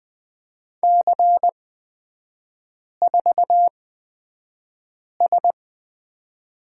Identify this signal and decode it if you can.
Morse code, heard as a keyed tone, about 20 wpm, spaced as for 5 wpm, 700 Hz. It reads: C4S
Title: 255 Lesson 41e - Period - Three Characters 20wpm